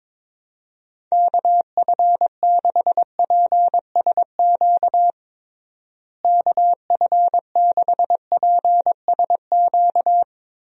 Morse code, heard as a keyed tone, about 22 wpm, 700 Hz. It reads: KF6PSQ KF6PSQ
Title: USA Call Signs - 6 Characters - Repeated 2x - 22wpm